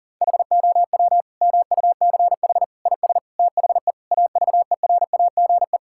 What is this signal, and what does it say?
Morse code, heard as a keyed tone, about 40 wpm, 700 Hz. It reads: HOW MUCH IS THE AVERAGE
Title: Sentences from Top 1000 Words - Review Part 6 - 40wpm